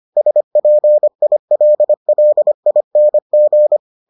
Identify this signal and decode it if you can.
Morse code, heard as a keyed tone, about 25 wpm, 600 Hz. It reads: SPILLING